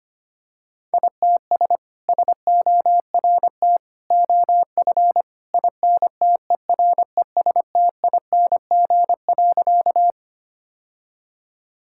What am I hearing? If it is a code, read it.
Morse code, heard as a keyed tone, about 25 wpm, 700 Hz. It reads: ITS SORT OF INTERESTING.
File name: practice / 25wpm / Sentences with 2-Word Phrases - 25wpm